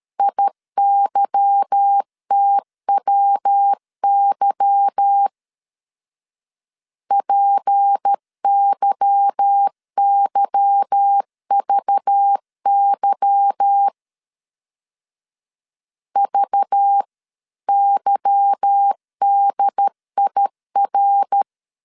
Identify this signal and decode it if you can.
Morse code, heard as a keyed tone, about 13 wpm, 800 Hz. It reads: IYTWY PYYVY VYDIR
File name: LEZIONI / 8 YJ45 / 8 Y 12wpm 2228